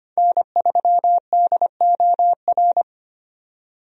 Morse code, heard as a keyed tone, about 25 wpm, 700 Hz. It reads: N3DOR
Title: USA Call Signs - 5 Characters - 25wpm